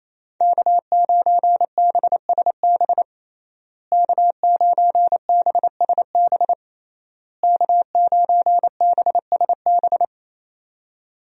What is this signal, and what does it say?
Morse code, heard as a keyed tone, about 27 wpm, 700 Hz. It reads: K9BSB K9BSB K9BSB
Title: USA Call Signs - 5 Characters - Repeated 3x - 28wpm